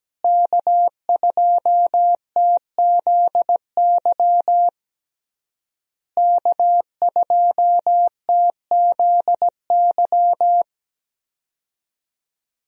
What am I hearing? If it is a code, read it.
Morse code, heard as a keyed tone, about 17 wpm, 700 Hz. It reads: K2TZY K2TZY
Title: USA Call Signs - 5 Characters - Repeated 3x - 17wpm